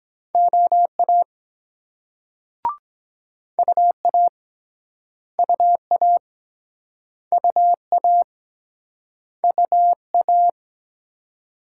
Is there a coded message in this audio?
Yes: Morse code from 0.3 to 10.5 s, heard as a keyed tone, about 23 wpm, 700 Hz. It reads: OA UA UA UA UA